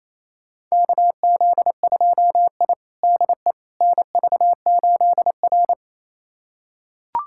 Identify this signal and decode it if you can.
Morse code, heard as a keyed tone, about 28 wpm, 700 Hz. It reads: KZ2I DE NV8R
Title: USA Callsign DE Callsign - 4 Characters - 28wpm